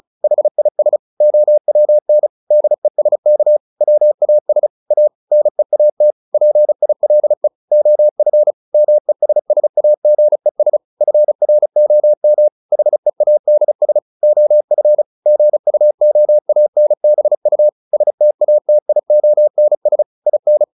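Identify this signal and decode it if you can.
Morse code, heard as a keyed tone, about 34 wpm, 600 Hz. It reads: HIS OWN DESK WAS A NEAT PILE OF MESSAGES FROM HEADS OF GUOANBU STATIONS IN
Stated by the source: Morse practice at 35 wpm from CW med Gnister, sending Ken Follett's novel Never in English.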